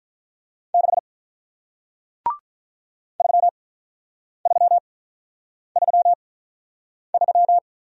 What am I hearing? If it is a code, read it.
Morse code, heard as a keyed tone, about 47 wpm, spaced as for 9 wpm, 700 Hz. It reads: 6 3333